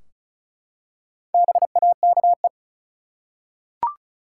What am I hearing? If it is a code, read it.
Morse code, heard as a keyed tone, about 36 wpm, 700 Hz. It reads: BAKE
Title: Four Letter Words - 35wpm